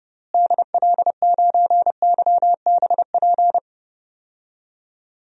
Morse code, heard as a keyed tone, about 30 wpm, 700 Hz. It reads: DL9YBP
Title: DXCC-100 Call Signs - 6 Characters - Repeated 2x - 30wpm